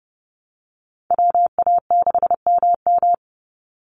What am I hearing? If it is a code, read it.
Morse code, heard as a keyed tone, about 30 wpm, 700 Hz. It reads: WA6MM
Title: NA SOTA Activator Callsigns 30wpm